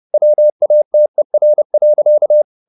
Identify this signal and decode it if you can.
Morse code, heard as a keyed tone, about 30 wpm, 600 Hz. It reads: WATER.
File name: the old man and the sea / the old man and the sea 1 30wpm